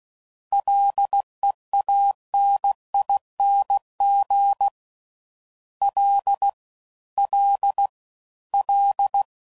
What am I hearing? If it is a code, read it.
Morse code, heard as a keyed tone, about 16 wpm, 800 Hz. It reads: LEANING L L L